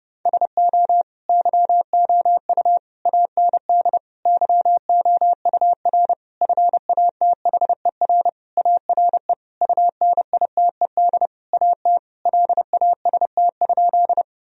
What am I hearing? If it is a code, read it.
Morse code, heard as a keyed tone, about 30 wpm, 700 Hz. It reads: SO YOU AND YOUR FATHER ARE UNITED AT LAST?
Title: Sentences from Top 1000 Words - Review Part 4 - 30wpm